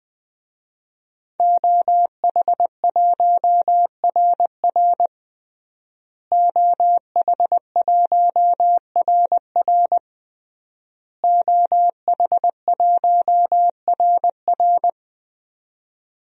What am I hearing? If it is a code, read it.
Morse code, heard as a keyed tone, about 20 wpm, 700 Hz. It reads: OH1RR OH1RR OH1RR
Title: DXCC-100 Call Signs - 5 Characters - Repeated 3x - 20wpm